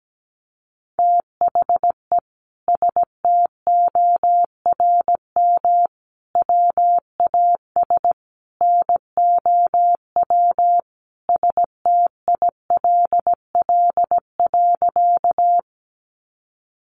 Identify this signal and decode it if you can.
Morse code, heard as a keyed tone, about 17 wpm, 700 Hz. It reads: THE STORM WAS NOW STILL.
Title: Sentences from Top 1500 Words 17wpm